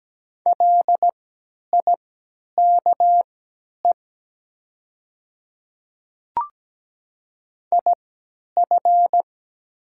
Morse code, heard as a keyed tone, about 17 wpm, spaced as for 10 wpm, 700 Hz. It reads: LIKE IF
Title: 3x Character Spacing - Top 100 Words - 17wpm